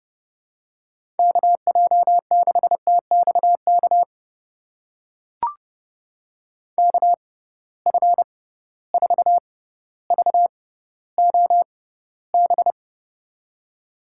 Morse code, heard as a keyed tone, about 30 wpm, spaced as for 15 wpm, 700 Hz. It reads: KJ6TXK KF4VOB
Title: USA Call Signs - 6 Characters - 6x Character Spacing - 30wpm